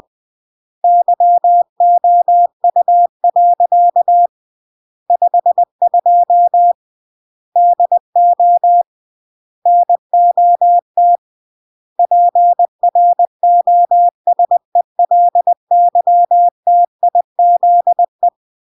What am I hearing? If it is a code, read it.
Morse code, heard as a keyed tone, about 20 wpm, 700 Hz. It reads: YOU. 52 DO NOT PROSELYTIZE
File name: CW / jodorowsky / 82 commandments 20-20